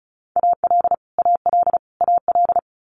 Morse code, heard as a keyed tone, about 36 wpm, 700 Hz. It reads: AL AL AL
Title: Multi-letter Phonemes Repeated 3x 35wpm